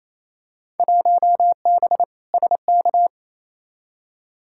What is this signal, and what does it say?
Morse code, heard as a keyed tone, about 28 wpm, 700 Hz. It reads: 1B SK
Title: Field Day Exchanges Encore - 28wpm